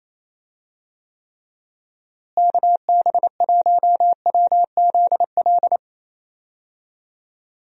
Morse code, heard as a keyed tone, about 27 wpm, 700 Hz. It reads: KB1WZL
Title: USA Call Signs - 6 Characters - Repeated 2x - 28wpm